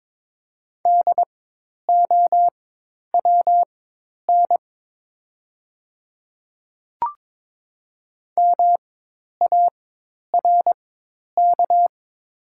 Morse code, heard as a keyed tone, about 22 wpm, spaced as for 10 wpm, 690 Hz. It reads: DOWN MARK
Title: 4x Character Spacing - Top 300 Words - 22wpm